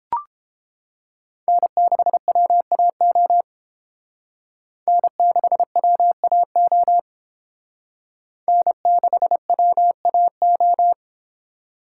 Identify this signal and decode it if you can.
Morse code, heard as a keyed tone, about 28 wpm, 690 Hz. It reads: N6WAO N6WAO N6WAO